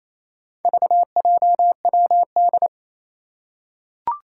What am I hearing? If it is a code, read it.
Morse code, heard as a keyed tone, about 27 wpm, 700 Hz. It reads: VJWD
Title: Four Challenging Letter-Number Combinations 28wpm